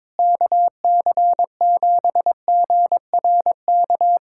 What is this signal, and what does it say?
Morse code, heard as a keyed tone, about 22 wpm, 700 Hz. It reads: KC7GRK
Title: USA Callsign DE Callsign - 6 Characters - 22wpm